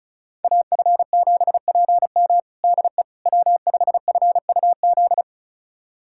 Morse code, heard as a keyed tone, about 36 wpm, 700 Hz. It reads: AF7PM DE W5FUZ
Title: USA Callsign DE Callsign - 5 Characters - 35wpm